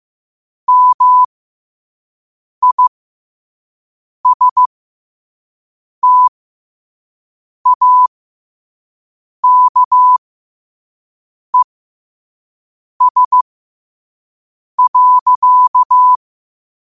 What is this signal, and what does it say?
Morse code, heard as a keyed tone, about 15 wpm, spaced as for 5 wpm, 1000 Hz. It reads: MISTAKES.
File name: qotdfiles / 5wpm